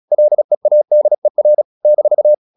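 Morse code, heard as a keyed tone, about 36 wpm, 600 Hz. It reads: LEADER -
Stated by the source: Morse practice at 36 wpm, text of Ken Follett's novel Never, from CW med Gnister.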